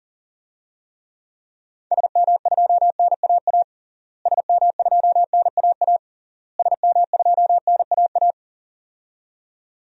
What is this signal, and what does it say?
Morse code, heard as a keyed tone, about 40 wpm, 700 Hz. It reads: SM2NAA SM2NAA SM2NAA
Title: DXCC-100 Call Signs - 6 Characters - Repeated 3x - 40wpm